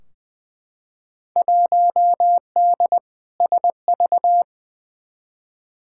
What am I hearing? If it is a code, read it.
Morse code, heard as a keyed tone, about 20 wpm, 700 Hz. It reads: 1D SV
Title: Field Day Exchanges - 20wpm